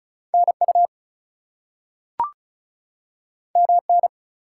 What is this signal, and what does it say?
Morse code, heard as a keyed tone, about 35 wpm, 700 Hz. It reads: NU MN
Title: US States and Canadian Provinces 35wpm